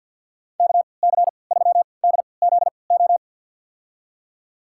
Morse code, heard as a keyed tone, about 51 wpm, spaced as for 28 wpm, 700 Hz. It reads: KC3DZO
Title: USA Call Signs - 3x Character Spacing - 50wpm